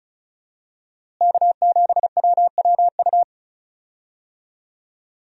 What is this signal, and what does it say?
Morse code, heard as a keyed tone, about 34 wpm, 700 Hz. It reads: K7WWU
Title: USA Call Signs - 5 Characters - Repeated 2x - 35wpm